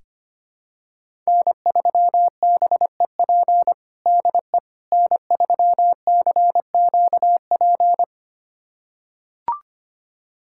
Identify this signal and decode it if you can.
Morse code, heard as a keyed tone, about 25 wpm, 700 Hz. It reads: N3BEP DE N3CQP
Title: USA Callsign DE Callsign - 5 Characters - 25wpm